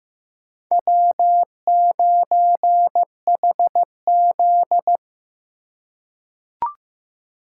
Morse code, heard as a keyed tone, about 15 wpm, 700 Hz. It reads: W9HZ
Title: USA Call Signs - 4 Characters - 2x Character Spacing - 15wpm